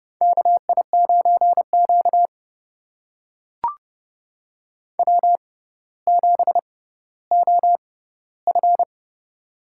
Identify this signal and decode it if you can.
Morse code, heard as a keyed tone, about 30 wpm, spaced as for 15 wpm, 700 Hz. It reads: KI9Q W7OF
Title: USA Call Signs - 4 Characters - 6x Character Spacing - 30wpm